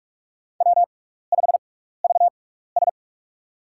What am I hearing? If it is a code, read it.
Morse code, heard as a keyed tone, about 42 wpm, spaced as for 15 wpm, 700 Hz. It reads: W5VS
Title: USA Call Signs - 4 Characters - 6x Character Spacing - 45wpm